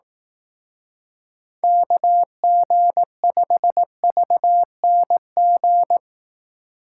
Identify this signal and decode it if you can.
Morse code, heard as a keyed tone, about 19 wpm, 700 Hz. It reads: KG5VNG
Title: Speed Racing - USA 2x3 Call Signs 15wpm